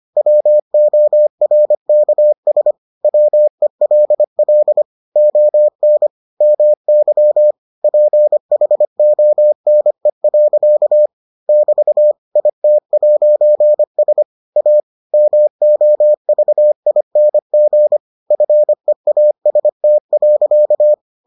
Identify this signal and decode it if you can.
Morse code, heard as a keyed tone, about 25 wpm, 600 Hz. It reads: WORKS WELL ON MY PHONE. <BT> IT'S A MOVING FEAST.